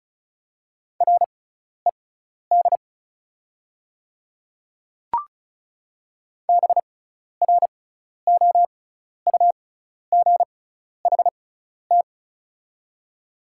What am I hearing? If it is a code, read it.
Morse code, heard as a keyed tone, about 36 wpm, spaced as for 12 wpm, 690 Hz. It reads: RED BROUGHT
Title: 6x Character Spacing - Top 500 Words - 35wpm